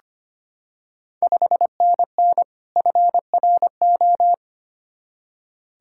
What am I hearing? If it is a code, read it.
Morse code, heard as a keyed tone, about 25 wpm, 690 Hz. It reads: 5NN FRO
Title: Ontario QSO Party Exchange 25wpm